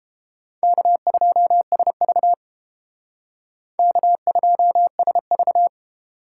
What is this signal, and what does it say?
Morse code, heard as a keyed tone, about 31 wpm, 700 Hz. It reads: K2SV K2SV